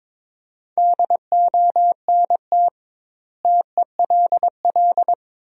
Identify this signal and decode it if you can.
Morse code, heard as a keyed tone, about 22 wpm, 700 Hz. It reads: DONT TELL
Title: 2x Word-Spacing - Sentences from Top 300 Words - 22wpm